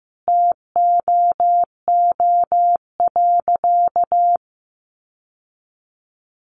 Morse code, heard as a keyed tone, about 15 wpm, 700 Hz. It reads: TOO.
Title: Sentences from Top 900 Words 15wpm